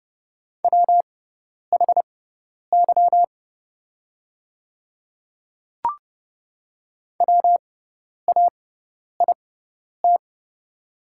Morse code, heard as a keyed tone, about 30 wpm, spaced as for 10 wpm, 700 Hz. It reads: WHY WAIT
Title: 6x Character Spacing - Top 500 Words - 30wpm